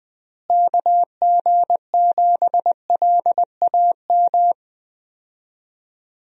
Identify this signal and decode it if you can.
Morse code, heard as a keyed tone, about 21 wpm, 700 Hz. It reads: KG7LAM
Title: USA Call Signs - 6 Characters - Repeated 3x - 20wpm